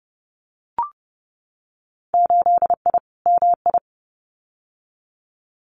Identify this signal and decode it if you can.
Morse code, heard as a keyed tone, about 30 wpm, 700 Hz. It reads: 8I MI
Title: Winter Field Day Exchanges Encore 30wpm